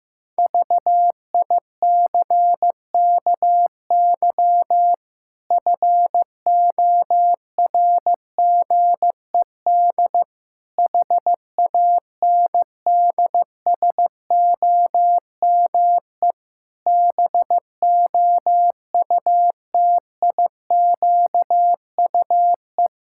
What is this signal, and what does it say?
Morse code, heard as a keyed tone, about 15 wpm, 700 Hz. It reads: VICKY FORGED HANDSOME BOUTIQUE
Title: Pangram Challenge v2 - 15wpm